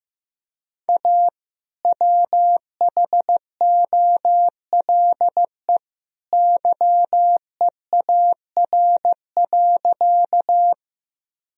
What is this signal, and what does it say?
Morse code, heard as a keyed tone, about 15 wpm, 700 Hz. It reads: A WHOLE YEAR.
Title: Sentences from Top 1000 Words - Review Part 1 - 15wpm